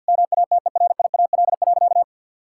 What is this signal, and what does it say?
Morse code, heard as a keyed tone, about 50 wpm, 700 Hz. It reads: MATERIAL.
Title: Sentences from Top 1000 Words - Review Part 5 - 50wpm